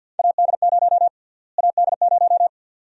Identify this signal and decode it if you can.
Morse code, heard as a keyed tone, about 49 wpm, 700 Hz. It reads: AD0 AD0
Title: USA Call Sign Prefixes - Repeated 2x - 50wpm